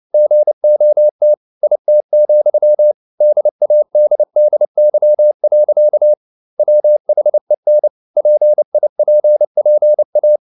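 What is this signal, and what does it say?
Morse code, heard as a keyed tone, about 29 wpm, 600 Hz. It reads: GOT IT, DADDY. WHEN PIPPA
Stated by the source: Morse code at 29 wpm, sending Ken Follett's novel Never in English, from CW med Gnister.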